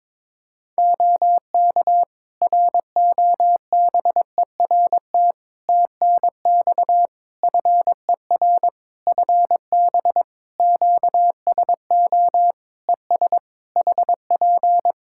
Morse code, heard as a keyed tone, about 22 wpm, 700 Hz. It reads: OK ROBERT TNX FER FB QSO ES HP
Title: Practice QSO - Exchange 4 - Ending - 22wpm